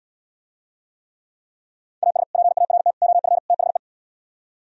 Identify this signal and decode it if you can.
Morse code, heard as a keyed tone, about 77 wpm, 700 Hz. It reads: AS PART OF THE